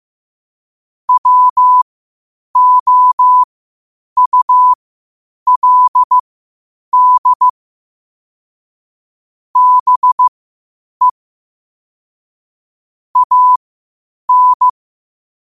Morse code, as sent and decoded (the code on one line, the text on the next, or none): .-- --- ..- .-.. -.. / -... / .- -.
WOULD BE AN